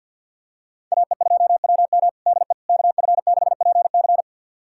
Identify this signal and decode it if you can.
Morse code, heard as a keyed tone, about 53 wpm, 700 Hz. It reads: AE2WM DE KF6PC
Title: USA Callsign DE Callsign - 5 Characters - 50wpm